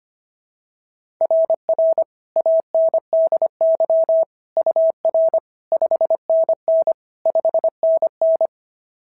Morse code, heard as a keyed tone, about 25 wpm, 690 Hz. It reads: RR ANDY UR 5NN 5NN
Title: SOTA QSO Practice 25wpm